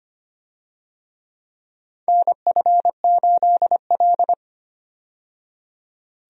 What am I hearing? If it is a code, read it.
Morse code, heard as a keyed tone, about 25 wpm, 700 Hz. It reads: NF8L
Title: USA Call Signs - 4 Characters - 25wpm